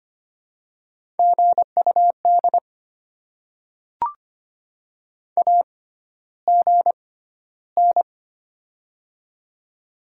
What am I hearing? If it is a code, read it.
Morse code, heard as a keyed tone, about 25 wpm, spaced as for 13 wpm, 700 Hz. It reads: GUD AGN
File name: Top 20 QSO Elements - 6x Character Spacing - 25wpm